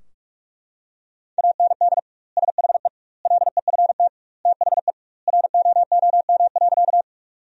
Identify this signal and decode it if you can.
Morse code, heard as a keyed tone, about 42 wpm, 700 Hz. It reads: AND SHE LEFT THE ROOM.
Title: 2x Word-Spacing - Sentences from Top 300 Words - 45wpm